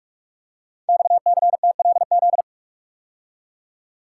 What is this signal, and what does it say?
Morse code, heard as a keyed tone, about 44 wpm, 690 Hz. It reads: XCTLZ
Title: Code Groups - 5 Letters - 45wpm